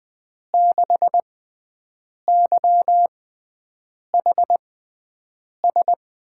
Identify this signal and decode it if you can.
Morse code, heard as a keyed tone, about 20 wpm, spaced as for 7 wpm, 700 Hz. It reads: 6YHS
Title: USA Call Signs - 6x Character Spacing - 20wpm